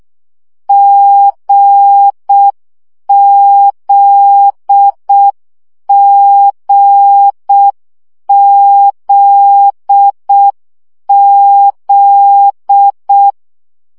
Morse code, heard as a keyed tone, about 6 wpm, 800 Hz. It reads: GZGZZ
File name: CW gyakorlas g z q 5 baud